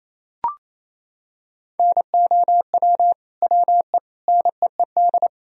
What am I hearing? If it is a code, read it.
Morse code, heard as a keyed tone, about 28 wpm, 700 Hz. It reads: NOW WE NEED
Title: Sentences from Top 1000 Words - Review Part 4 - 28wpm